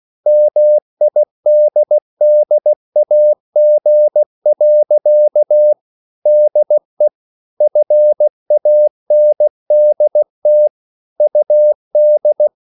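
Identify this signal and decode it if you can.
Morse code, heard as a keyed tone, about 16 wpm, 600 Hz. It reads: MIDDAG. DE FANDT UD
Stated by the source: Slow (QRS) Morse code at 16 wpm, from CW med Gnister.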